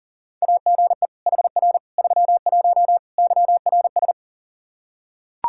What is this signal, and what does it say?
Morse code, heard as a keyed tone, about 40 wpm, 700 Hz. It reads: AGE HR 31 YRS
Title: Practice QSO - Exchange 3 - More Info - 40wpm